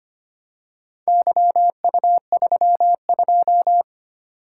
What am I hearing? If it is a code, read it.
Morse code, heard as a keyed tone, about 25 wpm, 700 Hz. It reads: YU32
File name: practice / 25wpm / Four Letter-Number Combinations - 25wpm